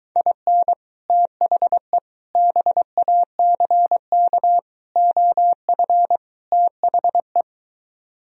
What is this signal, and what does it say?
Morse code, heard as a keyed tone, about 23 wpm, 700 Hz. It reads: IN THE BACK OF THE